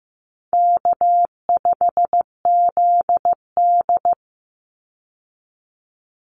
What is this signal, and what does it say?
Morse code, heard as a keyed tone, about 15 wpm, 700 Hz. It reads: K5ZD
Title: Easy Call Signs Repeated 3x - 15wpm